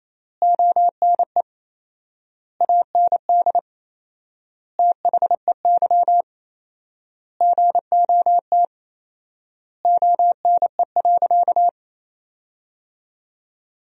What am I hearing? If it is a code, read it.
Morse code, heard as a keyed tone, about 27 wpm, 700 Hz. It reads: ONE AND THEY GOT ONE.
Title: 4x Word-Spacing - Sentences from Top 300 Words - 28wpm